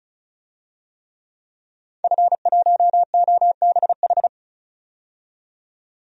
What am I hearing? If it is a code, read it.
Morse code, heard as a keyed tone, about 34 wpm, 700 Hz. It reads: F1OBH
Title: DXCC-100 Call Signs - 5 Characters - 35wpm